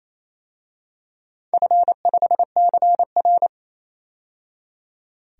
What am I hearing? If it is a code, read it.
Morse code, heard as a keyed tone, about 28 wpm, 700 Hz. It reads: F5CR
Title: DXCC-100 Call Signs - 4 Characters - 28wpm